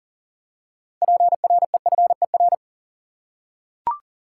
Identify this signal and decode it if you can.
Morse code, heard as a keyed tone, about 40 wpm, 700 Hz. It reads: PREFER
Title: Six Letter Words - 40wpm